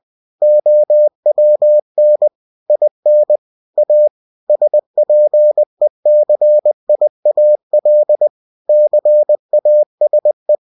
Morse code, heard as a keyed tone, about 20 wpm, 600 Hz. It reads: OWN IN A SPECIAL CASE
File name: cw-books / winnie-the-pooh / winnie-20wpm-page-0109